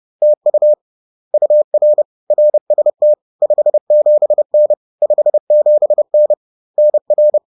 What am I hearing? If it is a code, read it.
Morse code, heard as a keyed tone, about 30 wpm, 600 Hz. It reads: TU UR RST 57N 57N NR